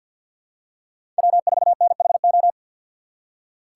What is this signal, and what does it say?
Morse code, heard as a keyed tone, about 48 wpm, 700 Hz. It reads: W4NHO
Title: USA Call Signs - 5 Characters - 4x Character Spacing - 50wpm